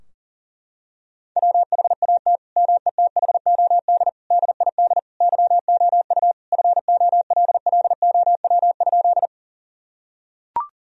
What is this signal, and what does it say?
Morse code, heard as a keyed tone, about 40 wpm, 700 Hz. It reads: WHAT METHOD DID YOU FOLLOW?
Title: Sentences from Top 1000 Words - Review Part 3 - 40wpm